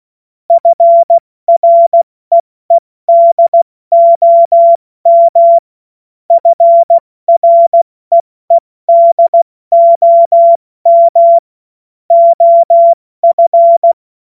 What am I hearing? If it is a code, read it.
Morse code, heard as a keyed tone, about 16 wpm, spaced as for 14 wpm, 680 Hz. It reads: FREEDOM FREEDOM OF